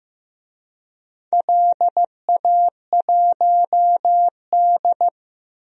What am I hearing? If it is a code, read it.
Morse code, heard as a keyed tone, about 16 wpm, 700 Hz. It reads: LA1D
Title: DXCC-100 Call Signs - 4 Characters - Repeated 3x - 15wpm